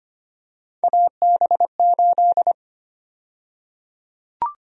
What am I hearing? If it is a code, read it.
Morse code, heard as a keyed tone, about 25 wpm, 700 Hz. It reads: AB8
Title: USA Call Sign Prefixes - 25wpm